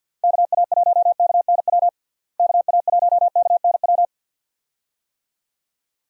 Morse code, heard as a keyed tone, about 51 wpm, 700 Hz. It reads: KA1KNW KA1KNW
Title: USA Call Signs - 6 Characters - Repeated 2x - 50wpm